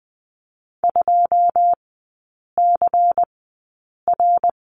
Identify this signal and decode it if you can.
Morse code, heard as a keyed tone, about 20 wpm, spaced as for 8 wpm, 700 Hz. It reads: 2CR
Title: 100 Lesson 17e - Letter u - Three Characters 20wpm